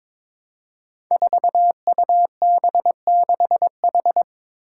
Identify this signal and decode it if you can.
Morse code, heard as a keyed tone, about 22 wpm, 700 Hz. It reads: 4UB6H
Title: Five Challenging Letter-Number Combinations 22wpm